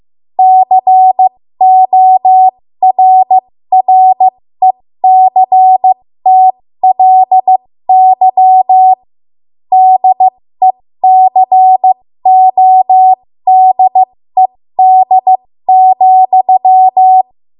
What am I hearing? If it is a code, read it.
Morse code, heard as a keyed tone, about 15 wpm, spaced as for 13 wpm, 750 Hz. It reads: CORRECTLY DECODED,
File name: Morse / 13WPM